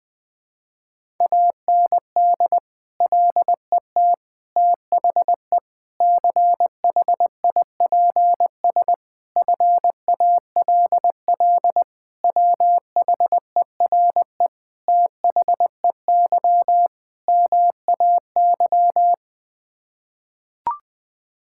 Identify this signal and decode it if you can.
Morse code, heard as a keyed tone, about 21 wpm, 700 Hz. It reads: AND LET THE CHIPS FALL WHERE THEY MAY
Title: Sets of 8 Words - 20wpm